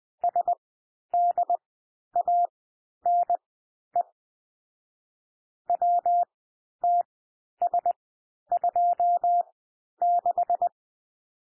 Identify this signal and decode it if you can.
Morse code, heard as a keyed tone, about 20 wpm, spaced as for 11 wpm, 700 Hz. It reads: SDANE WTS26
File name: cwmp3 / M1 SOUND MIXED 20 wpm